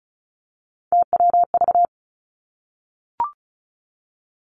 Morse code, heard as a keyed tone, about 36 wpm, 700 Hz. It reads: TWV
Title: Three Challenging Letter-Number Combinations 35wpm